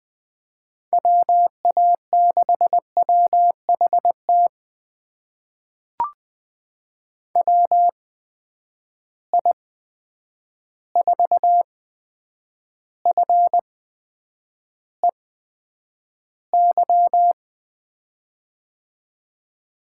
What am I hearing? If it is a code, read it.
Morse code, heard as a keyed tone, about 20 wpm, spaced as for 9 wpm, 700 Hz. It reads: WA6WHT WI4FEY
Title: USA Call Signs - 6 Characters - 8x Character Spacing - 20wpm